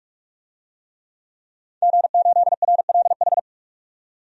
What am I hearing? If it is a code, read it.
Morse code, heard as a keyed tone, about 46 wpm, 700 Hz. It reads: G8RLH
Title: DXCC-100 Call Signs - 5 Characters - 45wpm